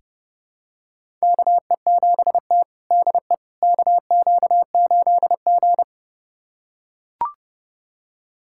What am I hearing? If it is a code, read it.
Morse code, heard as a keyed tone, about 30 wpm, 700 Hz. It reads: KE7T DE KQ8G
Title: USA Callsign DE Callsign - 4 Characters - 30wpm